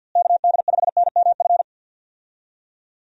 Morse code, heard as a keyed tone, about 50 wpm, 700 Hz. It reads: KD5NMF
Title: USA Call Signs - 6 Characters - Repeated 3x - 50wpm